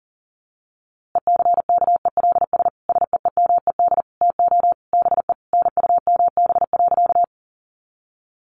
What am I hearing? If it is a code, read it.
Morse code, heard as a keyed tone, about 40 wpm, 700 Hz. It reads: ECKELS SEEMED TO BE NUMB.
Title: A Sound Of Thunder 40wpm